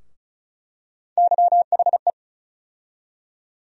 Morse code, heard as a keyed tone, about 34 wpm, 700 Hz. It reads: YHE
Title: USA Call Sign Postfixes - 3 Letters 35wpm